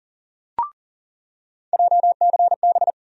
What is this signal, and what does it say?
Morse code, heard as a keyed tone, about 40 wpm, 700 Hz. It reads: JCB
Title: USA Call Sign Postfixes - 3 Letters 40wpm